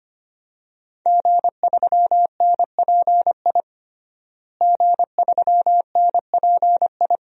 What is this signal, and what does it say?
Morse code, heard as a keyed tone, about 25 wpm, 700 Hz. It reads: G3NPI G3NPI
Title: DXCC-100 Call Signs - 5 Characters - Repeated 2x - 25wpm